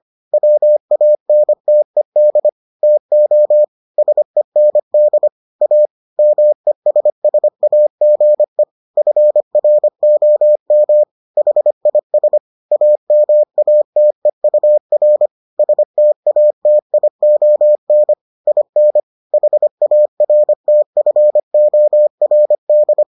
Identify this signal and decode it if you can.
Morse code, heard as a keyed tone, about 25 wpm, 600 Hz. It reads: WANTED TO SEND A MESSAGE FROM HIS AMATEUR STATION IN HARTFORD